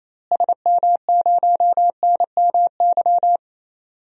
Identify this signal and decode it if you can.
Morse code, heard as a keyed tone, about 28 wpm, 700 Hz. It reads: SM0NMY
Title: DXCC-100 Call Signs - 6 Characters - Repeated 2x - 28wpm